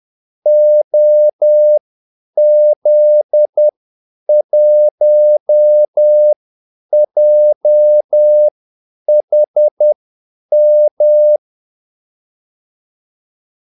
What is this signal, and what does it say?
Morse code, heard as a keyed tone, about 10 wpm, spaced as for 8 wpm, 600 Hz. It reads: OZ1JHM
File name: Callsigns-10-8wpm